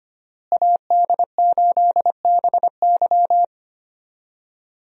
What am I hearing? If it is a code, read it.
Morse code, heard as a keyed tone, about 25 wpm, 700 Hz. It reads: AD8BY
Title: USA Call Signs - 5 Characters - Repeated 2x - 25wpm